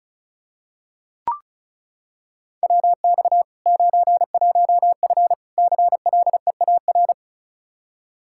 Morse code, heard as a keyed tone, about 36 wpm, 700 Hz. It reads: WX 91F CLEAR
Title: Practice QSO - Exchange 2 - Info - 35wpm